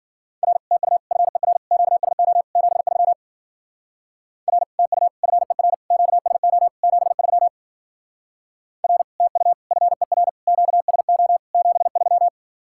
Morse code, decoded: R TU FER QSO 73 R TU FER QSO 73 R TU FER QSO 73

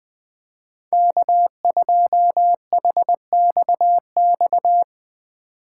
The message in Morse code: K2HXX

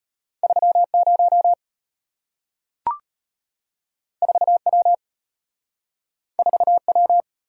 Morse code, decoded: 30 4W 4W

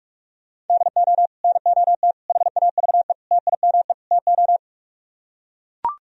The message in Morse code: DO NOT HAVE TIME TO

